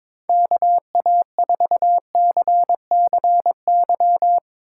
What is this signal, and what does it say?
Morse code, heard as a keyed tone, about 24 wpm, 700 Hz. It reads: KA4CCY
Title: Speed Racing - USA 2x3 Call Signs 22wpm